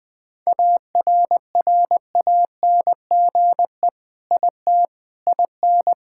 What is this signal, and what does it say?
Morse code, heard as a keyed tone, about 20 wpm, 700 Hz. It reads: ARRANGE IT IN
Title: Sentences with 5-Word Phrases - Spoken Beginning - 20wpm